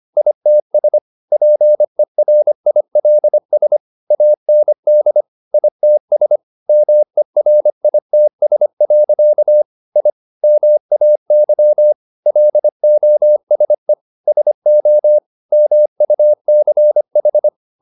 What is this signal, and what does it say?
Morse code, heard as a keyed tone, about 25 wpm, 600 Hz. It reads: ITS PERILS AND ITS MERITS. I MAY LOSE SO MUCH